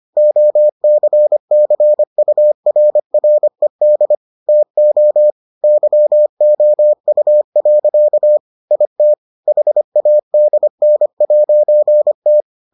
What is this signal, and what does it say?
Morse code, heard as a keyed tone, about 25 wpm, 600 Hz. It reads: OCCURRED TO YOU. IT HADN'T